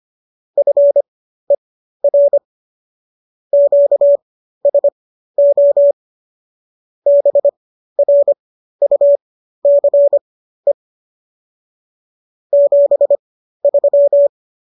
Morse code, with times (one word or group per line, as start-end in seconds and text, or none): 0.6-2.4 FER
3.5-5.9 QSO
7.1-10.7 BRUCE
12.5-14.3 73